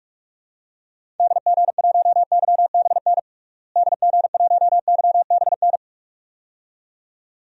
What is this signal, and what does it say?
Morse code, heard as a keyed tone, about 42 wpm, 700 Hz. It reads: DG1YBN DG1YBN
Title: DXCC-100 Call Signs - 6 Characters - Repeated 2x - 45wpm